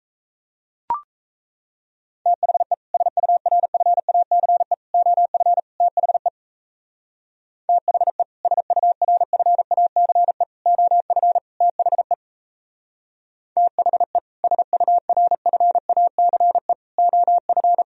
Morse code, heard as a keyed tone, about 34 wpm, 700 Hz. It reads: THE SURFACE OF THE THE SURFACE OF THE THE SURFACE OF
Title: Speed-Racing - Sets of 4 Words 28wpm